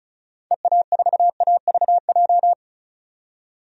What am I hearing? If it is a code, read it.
Morse code, heard as a keyed tone, about 34 wpm, 700 Hz. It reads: EA4AVJ
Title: DXCC-100 Call Signs - 6 Characters - 35wpm